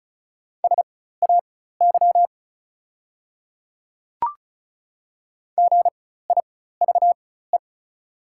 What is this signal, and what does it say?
Morse code, heard as a keyed tone, about 36 wpm, spaced as for 16 wpm, 690 Hz. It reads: SAY GIVE
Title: 4x Character Spacing - Top 200 Words - 35wpm